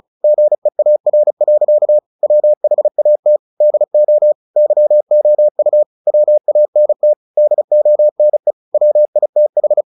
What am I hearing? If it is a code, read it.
Morse code, heard as a keyed tone, about 34 wpm, 600 Hz. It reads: GEAR. WHAT DO YOU WANT DONE WITH
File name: the old man and the sea 2 35wpm